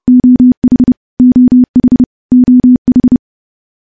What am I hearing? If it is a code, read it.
Morse code, heard as a keyed tone, about 28 wpm, 250 Hz. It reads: OH OH OH
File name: morse / morse-training-30WPM